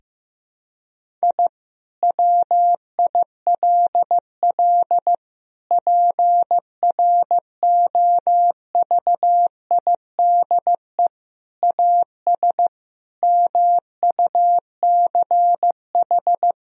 Morse code, heard as a keyed tone, about 15 wpm, 700 Hz. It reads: I WILL PROVIDE AS MUCH
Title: Sentences from Top 1000 Words - Review Part 3 - 15wpm